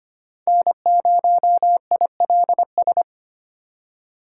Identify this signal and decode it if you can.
Morse code, heard as a keyed tone, about 25 wpm, 700 Hz. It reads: N0ILS
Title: USA Call Signs - 5 Characters - 3x Character Spacing - 25wpm